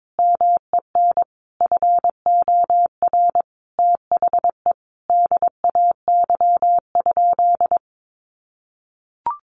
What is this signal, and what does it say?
Morse code, heard as a keyed tone, about 22 wpm, 700 Hz. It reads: MEN FOR THE DAY?
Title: Sentences from Top 1400 Words 22wpm